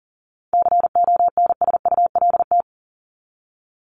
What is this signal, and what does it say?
Morse code, heard as a keyed tone, about 39 wpm, 700 Hz. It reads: CONSULT
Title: Top 1751-2000 Words 40wpm